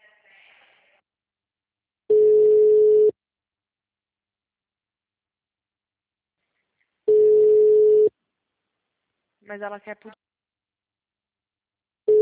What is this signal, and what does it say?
Ringback tone (Europe, 425 Hz), with speech.
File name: out- -2002- - - .24